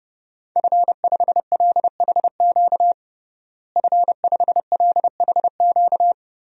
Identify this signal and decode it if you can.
Morse code, heard as a keyed tone, about 30 wpm, 700 Hz. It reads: F5LHQ F5LHQ
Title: DXCC-100 Call Signs - 5 Characters - Repeated 3x - 30wpm